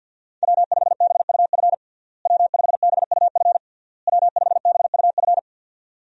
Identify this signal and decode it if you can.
Morse code, heard as a keyed tone, about 51 wpm, 700 Hz. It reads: W5BUF W5BUF W5BUF
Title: USA Call Signs - 5 Characters - Repeated 3x - 50wpm